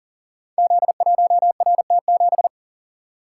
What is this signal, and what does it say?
Morse code, heard as a keyed tone, about 40 wpm, 700 Hz. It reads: Z1RT7
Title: Code Groups - 5 Letter:Numbers - 40wpm